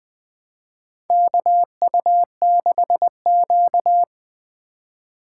Morse code, heard as a keyed tone, about 20 wpm, 700 Hz. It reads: KU6Q
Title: USA Call Signs - 4 Characters - 2x Character Spacing - 20wpm